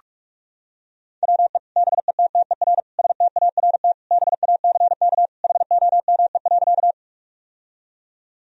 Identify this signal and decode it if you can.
Morse code, heard as a keyed tone, about 46 wpm, 700 Hz. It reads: WE BETTER START BACK HOME.
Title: Sentences from Top 1000 Words - Review Part 1 - 45wpm